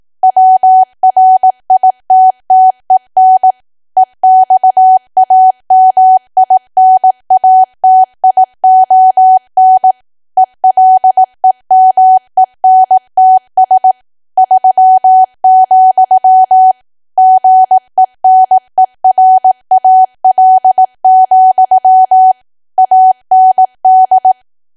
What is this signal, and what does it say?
Morse code, heard as a keyed tone, about 18 wpm, 750 Hz. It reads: WRITTEN EXAMINATION ELEMENTS 3, GENERAL, AND
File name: cwa / W1AW-Level2 / 18WPM